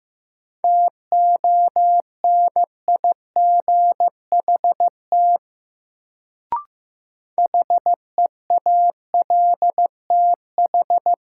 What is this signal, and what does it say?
Morse code, heard as a keyed tone, about 15 wpm, 700 Hz. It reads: TONIGHT HEALTH
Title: Sentences from Top 600 Words - New Word Drill - 15wpm